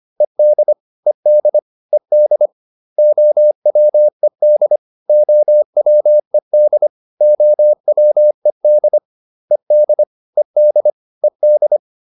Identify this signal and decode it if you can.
Morse code, heard as a keyed tone, about 25 wpm, 600 Hz. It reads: ED ED ED OWED OWED OWED ED ED ED